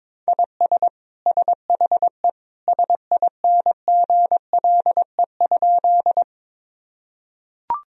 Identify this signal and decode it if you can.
Morse code, heard as a keyed tone, about 22 wpm, 700 Hz. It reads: IS SHE SINGLE?
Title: Sentences from Top 1000 Words - Review Part 4 - 22wpm